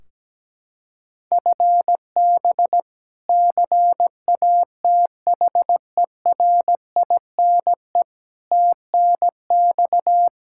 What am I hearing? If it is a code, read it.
Morse code, heard as a keyed tone, about 17 wpm, 700 Hz. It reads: FB CATHERINE TNX
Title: K1USN SST QSO Practice 17wpm